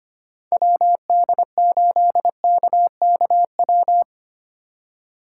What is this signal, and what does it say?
Morse code, heard as a keyed tone, about 30 wpm, 700 Hz. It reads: WD8KKW
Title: Speed Racing - USA 2x3 Call Signs 25wpm